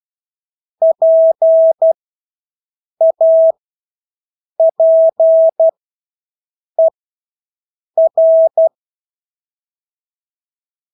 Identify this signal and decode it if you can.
Morse code, heard as a keyed tone, about 12 wpm, spaced as for 6 wpm, 650 Hz. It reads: PAPER
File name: Engelske ord 12-6 wpm